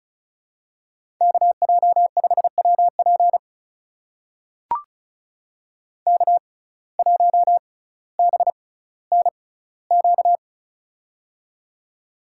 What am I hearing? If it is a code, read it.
Morse code, heard as a keyed tone, about 36 wpm, spaced as for 18 wpm, 700 Hz. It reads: KJ5WP K1BNQ
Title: USA Call Signs - 5 Characters - 6x Character Spacing - 35wpm